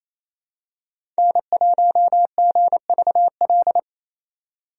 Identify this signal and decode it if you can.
Morse code, heard as a keyed tone, about 28 wpm, 700 Hz. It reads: N1GVL
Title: USA Call Signs - 5 Characters - 28wpm